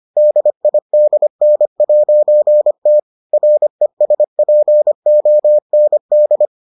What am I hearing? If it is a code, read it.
Morse code, heard as a keyed tone, about 25 wpm, 600 Hz. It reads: DIDN'T RESPOND